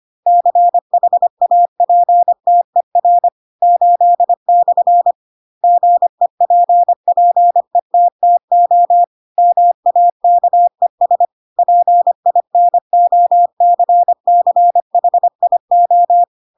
Morse code, heard as a keyed tone, about 25 wpm, 700 Hz. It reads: CHAPTER 8/ GEPPETTO MAKES PINOCCHIO
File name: Pinocchio / Pinocchio ch8 25 25 ik1hge